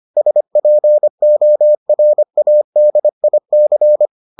Morse code, heard as a keyed tone, about 25 wpm, 600 Hz. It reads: SPORADIC